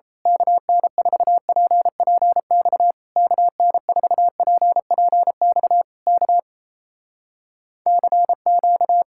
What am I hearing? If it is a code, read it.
Morse code, heard as a keyed tone, about 27 wpm, 700 Hz. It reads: KN4PPX KN4PPX K CQ
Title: Speed-Racing Common QSO Phrases - 28wpm